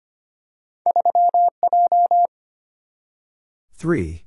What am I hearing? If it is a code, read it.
Morse code, heard as a keyed tone, about 25 wpm, 700 Hz. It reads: 3J
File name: Two Letter-Number Combinations - 25wpm